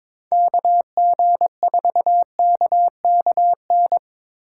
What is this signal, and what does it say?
Morse code, heard as a keyed tone, about 22 wpm, 700 Hz. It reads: KG4KKN
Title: USA Call Signs - 6 Characters - 4x Character Spacing - 22wpm